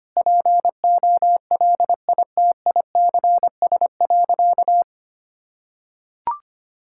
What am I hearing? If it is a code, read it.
Morse code, heard as a keyed tone, about 25 wpm, 700 Hz. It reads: POLITICS.
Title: Sentences from Top 1000 Words - Review Part 6 - 25wpm